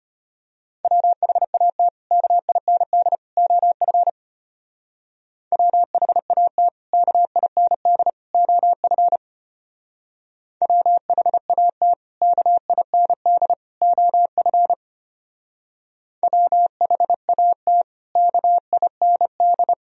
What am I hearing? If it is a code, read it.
Morse code, heard as a keyed tone, about 31 wpm, 700 Hz. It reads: WHAT KIND OF WHAT KIND OF WHAT KIND OF WHAT KIND
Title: Speed-Racing - Sets of 3 Words 25wpm